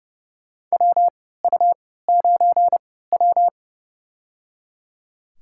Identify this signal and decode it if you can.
Morse code, heard as a keyed tone, about 30 wpm, spaced as for 17 wpm, 700 Hz. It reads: WU9W
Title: USA Call Signs - 4 Characters - 3x Character Spacing - 30wpm